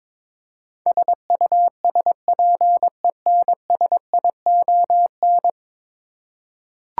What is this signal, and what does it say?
Morse code, heard as a keyed tone, about 21 wpm, 700 Hz. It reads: SUSPENSION
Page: morse-code-ninja-practice-22wpm